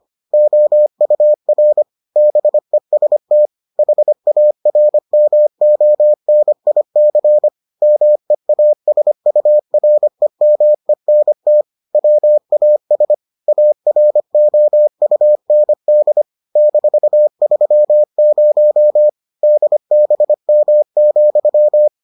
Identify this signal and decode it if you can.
Morse code, heard as a keyed tone, about 25 wpm, 600 Hz. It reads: OUR BEST HARMONIC MEASUREMENT WAS AROUND -30 DBM,